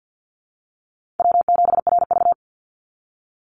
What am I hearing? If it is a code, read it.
Morse code, heard as a keyed tone, about 50 wpm, 700 Hz. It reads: W7RV
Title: NA SOTA Chaser Callsigns 50wpm